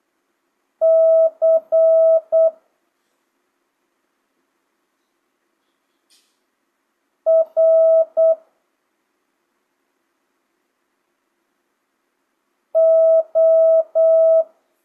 Morse code, heard as a keyed tone, about 8 wpm, 650 Hz. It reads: C R O
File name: jh8cbh / 2025cw